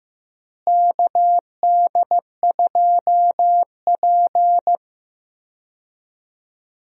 Morse code, heard as a keyed tone, about 15 wpm, 700 Hz. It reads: KD2P
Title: USA Call Signs - 4 Characters - Repeated 3x - 15wpm